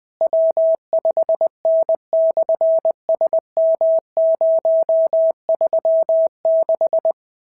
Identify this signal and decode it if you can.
Morse code, heard as a keyed tone, about 20 wpm, 690 Hz. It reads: W5N/SM036
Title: SOTA QSO Practice 20wpm